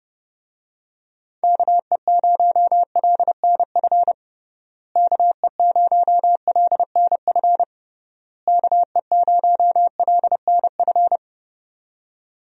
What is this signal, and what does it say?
Morse code, heard as a keyed tone, about 30 wpm, 700 Hz. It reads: KE0LNF KE0LNF KE0LNF
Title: USA Call Signs - 6 Characters - Repeated 3x - 30wpm